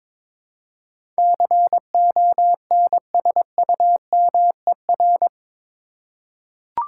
Morse code, heard as a keyed tone, about 22 wpm, 700 Hz. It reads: CONSUMER
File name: practice / 22wpm / Eight Letter Words - 22wpm